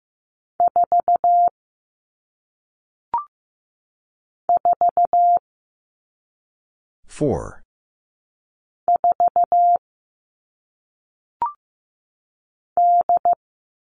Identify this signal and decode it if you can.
Morse code, heard as a keyed tone, about 15 wpm, 700 Hz. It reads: 4 4 4 D
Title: All Cut Numbers - 1-Digit 15wpm